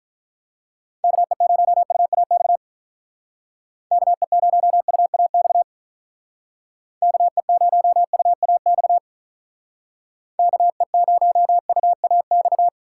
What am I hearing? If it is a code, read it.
Morse code, heard as a keyed tone, about 43 wpm, 700 Hz. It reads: KE0UAX KE0UAX KE0UAX KE0UAX
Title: Speed Racing - USA 2x3 Call Signs 35wpm